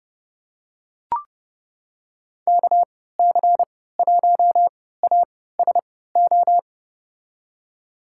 Morse code, heard as a keyed tone, about 30 wpm, spaced as for 17 wpm, 700 Hz. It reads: KC1ASO
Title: USA Call Signs - 6 Characters - 3x Character Spacing - 30wpm